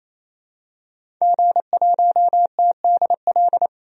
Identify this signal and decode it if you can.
Morse code, heard as a keyed tone, about 27 wpm, 690 Hz. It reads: G1TDL
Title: DXCC-100 Call Signs - 5 Characters - 28wpm